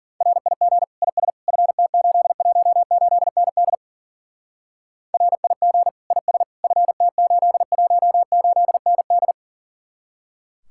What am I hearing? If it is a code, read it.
Morse code, heard as a keyed tone, about 50 wpm, 700 Hz. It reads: RIG IS FT818ND RIG IS FT818ND
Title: Speed-Racing Common QSO Phrases Encore - 40wpm